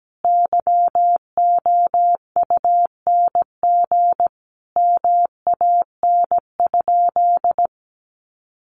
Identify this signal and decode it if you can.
Morse code, heard as a keyed tone, about 17 wpm, 700 Hz. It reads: YOUNG MAN?
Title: Sentences from Top 400 Words 17wpm